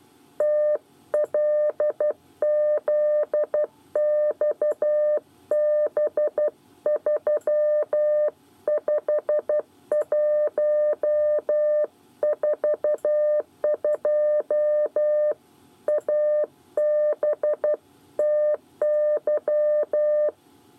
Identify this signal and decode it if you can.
Morse code, heard as a keyed tone, about 12 wpm, 600 Hz. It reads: TLZXB 35142 ABTY
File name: 8 bis lezione cw 1